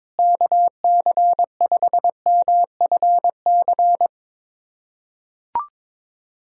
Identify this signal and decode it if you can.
Morse code, heard as a keyed tone, about 22 wpm, 700 Hz. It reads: KC5MFC
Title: USA Callsign DE Callsign - 6 Characters - 22wpm